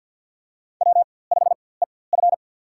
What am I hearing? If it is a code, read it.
Morse code, heard as a keyed tone, about 48 wpm, spaced as for 23 wpm, 700 Hz. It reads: W5EF
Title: USA Call Signs - 4x Character Spacing - 50wpm